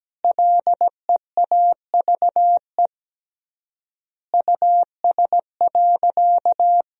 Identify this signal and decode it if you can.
Morse code, heard as a keyed tone, about 17 wpm, 700 Hz. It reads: LEAVE US.
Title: 3x Word-Spacing - Sentences from Top 400 Words - 17wpm